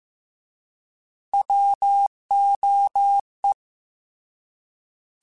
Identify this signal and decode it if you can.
Morse code, heard as a keyed tone, about 15 wpm, 800 Hz. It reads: WOE
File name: Alphabetical / Alphabetical mp3s / Ws @13